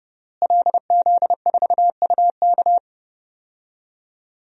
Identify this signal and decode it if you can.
Morse code, heard as a keyed tone, about 30 wpm, 700 Hz. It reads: LZ4UK
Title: DXCC-100 Call Signs - 5 Characters - Repeated 3x - 30wpm